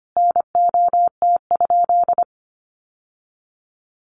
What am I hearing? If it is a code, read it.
Morse code, heard as a keyed tone, about 25 wpm, 700 Hz. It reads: NOT?
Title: Sentences from Top 600 Words 25wpm